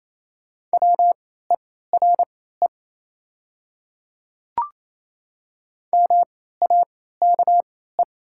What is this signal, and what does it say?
Morse code, heard as a keyed tone, about 28 wpm, spaced as for 16 wpm, 690 Hz. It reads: WERE MAKE
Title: 3x Character Spacing - Top 100 Words - 28wpm